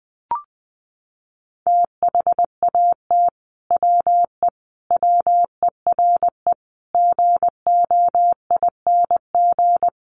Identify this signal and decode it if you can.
Morse code, heard as a keyed tone, about 19 wpm, 700 Hz. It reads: THAT WE WERE GOING
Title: Sets of 5 Words Encore2 20wpm